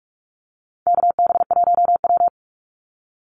Morse code, heard as a keyed tone, about 44 wpm, 700 Hz. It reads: KB1W
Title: Call Signs Repeated 3x - 45wpm